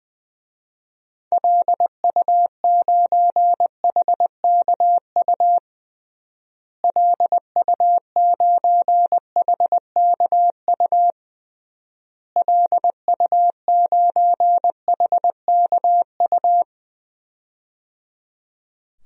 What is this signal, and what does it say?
Morse code, heard as a keyed tone, about 20 wpm, 700 Hz. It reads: LU9HKU LU9HKU LU9HKU
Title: DXCC-100 Call Signs - 6 Characters - Repeated 3x - 20wpm